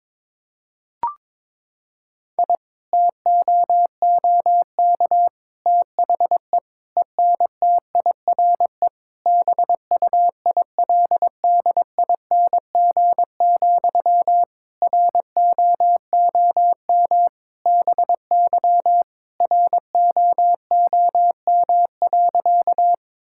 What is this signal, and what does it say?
Morse code, heard as a keyed tone, about 21 wpm, 700 Hz. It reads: I TOOK THE ENTIRE BUILDING, ROOM BY ROOM.
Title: Sentences from Top 1000 Words - Review Part 5 - 22wpm